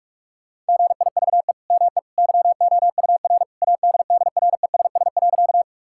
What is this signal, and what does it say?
Morse code, heard as a keyed tone, about 44 wpm, 700 Hz. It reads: GIVE ME YOUR ADDRESS.
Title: Sentences from Top 1000 Words - Review Part 6 - 45wpm